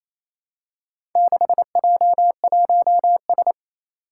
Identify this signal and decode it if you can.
Morse code, heard as a keyed tone, about 27 wpm, 700 Hz. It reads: 6J1S
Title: Four Challenging Letter-Number Combinations 28wpm